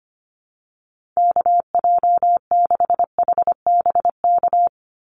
Morse code, heard as a keyed tone, about 25 wpm, 700 Hz. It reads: KJ6HBK
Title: USA Call Signs 25wpm